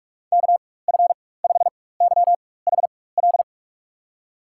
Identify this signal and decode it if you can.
Morse code, heard as a keyed tone, about 43 wpm, spaced as for 21 wpm, 700 Hz. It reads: KF5YHL
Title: USA Call Signs - 6 Characters - 4x Character Spacing - 45wpm